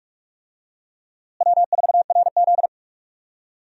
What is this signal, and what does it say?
Morse code, heard as a keyed tone, about 46 wpm, 700 Hz. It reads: W4RZ
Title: USA Call Signs - 4 Characters - Repeated 3x - 45wpm